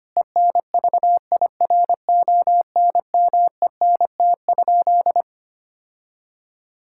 Morse code, heard as a keyed tone, about 25 wpm, 700 Hz. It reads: ENVIRONMENT?
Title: Sentences from Top 1000 Words - Review Part 5 - 25wpm